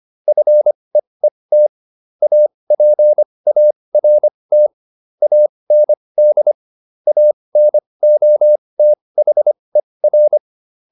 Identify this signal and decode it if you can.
Morse code, heard as a keyed tone, about 25 wpm, spaced as for 20 wpm, 600 Hz. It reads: FEET APART AND ANOTHER